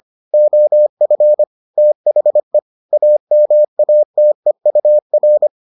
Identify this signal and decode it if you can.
Morse code, heard as a keyed tone, about 25 wpm, 600 Hz. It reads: OF THE AMATEUR